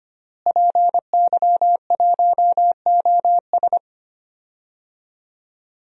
Morse code, heard as a keyed tone, about 25 wpm, 700 Hz. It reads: PY1OS
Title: DXCC-100 Call Signs - 5 Characters - Repeated 3x - 25wpm